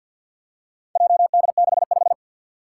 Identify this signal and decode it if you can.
Morse code, heard as a keyed tone, about 52 wpm, 700 Hz. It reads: JD65
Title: Four Challenging Letter-Number Combinations 50wpm